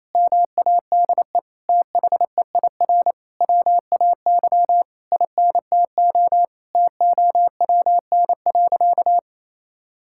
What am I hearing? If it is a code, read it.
Morse code, heard as a keyed tone, about 28 wpm, 700 Hz. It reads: MADE THEIR WAY INTO TOWN.
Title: Sentences from Top 1000 Words - Review Part 6 - 28wpm